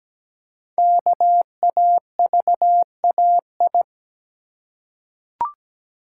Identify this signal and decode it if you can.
Morse code, heard as a keyed tone, about 17 wpm, 700 Hz. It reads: KAVAI
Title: Code Groups - 5 Letters - 17wpm